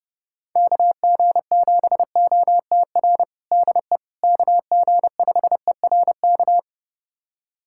Morse code, heard as a keyed tone, about 30 wpm, 700 Hz. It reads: KG7OTR DE KG5ERK
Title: USA Callsign DE Callsign - 6 Characters - 30wpm